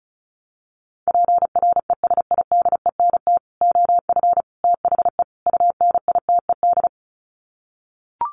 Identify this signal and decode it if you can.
Morse code, heard as a keyed tone, about 34 wpm, 700 Hz. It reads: PRESIDENT OF THE UNITED
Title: Sets of 4 Words 35wpm